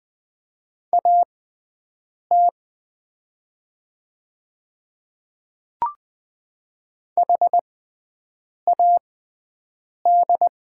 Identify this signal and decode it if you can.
Morse code, heard as a keyed tone, about 20 wpm, spaced as for 7 wpm, 700 Hz. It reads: AT HAD
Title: 6x Character Spacing - Top 100 Words - 20wpm